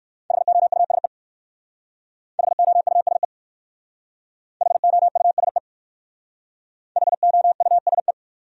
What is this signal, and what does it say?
Morse code, heard as a keyed tone, about 58 wpm, 700 Hz. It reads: HOUSE HOUSE HOUSE HOUSE